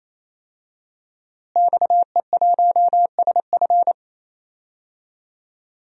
Morse code, heard as a keyed tone, about 28 wpm, 700 Hz. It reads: XE1SF
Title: DXCC-100 Call Signs - 5 Characters - Repeated 3x - 28wpm